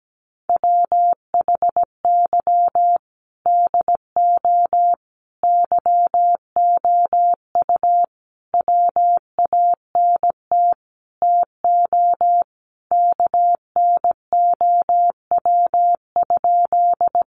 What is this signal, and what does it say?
Morse code, heard as a keyed tone, about 17 wpm, 700 Hz. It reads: WHY DO YOU WANT TO KNOW?
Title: Sentences from Top 200 Words 17wpm